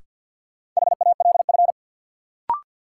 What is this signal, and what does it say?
Morse code, heard as a keyed tone, about 53 wpm, 700 Hz. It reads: HALF
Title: Sets of 3 Words - New Word Drill - 50wpm